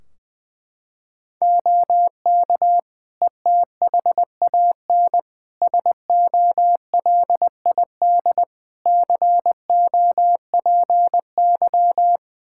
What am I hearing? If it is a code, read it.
Morse code, heard as a keyed tone, about 20 wpm, 700 Hz. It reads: OK ETHAN SOLID COPY
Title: Practice QSO - Exchange 3 - More Info - 20wpm